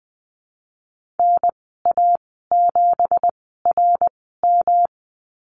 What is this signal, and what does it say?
Morse code, heard as a keyed tone, about 20 wpm, spaced as for 15 wpm, 700 Hz. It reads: NA7RM
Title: 238 Lesson 38g - Letter q - Call Signs 20wpm